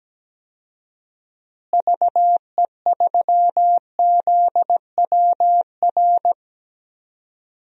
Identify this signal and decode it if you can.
Morse code, heard as a keyed tone, about 17 wpm, 700 Hz. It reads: VE3ZWR
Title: DXCC-100 Call Signs - 6 Characters - Repeated 3x - 17wpm